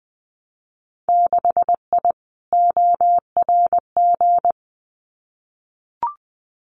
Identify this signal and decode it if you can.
Morse code, heard as a keyed tone, about 20 wpm, 700 Hz. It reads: 6I ORG
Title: Winter Field Day Exchanges Encore 20wpm